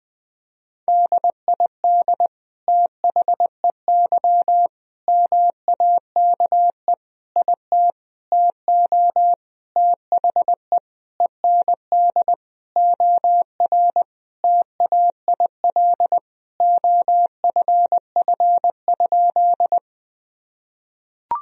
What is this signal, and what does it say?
Morse code, heard as a keyed tone, about 20 wpm, 700 Hz. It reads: DID THEY MAKE IT TO THE END OR TAIL OFF?
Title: Sentences from Top 1000 Words - Review Part 3 - 20wpm